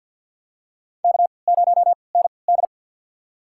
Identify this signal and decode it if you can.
Morse code, heard as a keyed tone, about 49 wpm, spaced as for 28 wpm, 700 Hz. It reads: K0ND
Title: USA Call Signs - 4 Characters - 3x Character Spacing - 50wpm